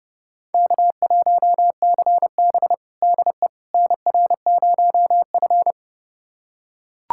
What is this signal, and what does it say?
Morse code, heard as a keyed tone, about 32 wpm, 700 Hz. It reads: K1CB DE NR0F
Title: USA Callsign DE Callsign - 4 Characters - 30wpm